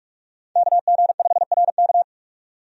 Morse code, heard as a keyed tone, about 46 wpm, 700 Hz. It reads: KG5RK
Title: USA Call Signs - 5 Characters - 45wpm